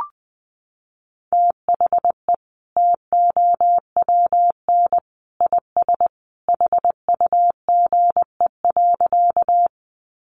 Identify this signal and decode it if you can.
Morse code, heard as a keyed tone, about 20 wpm, 700 Hz. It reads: THE TOWN IS HUGE.
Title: Sentences from Top 1100 Words 20wpm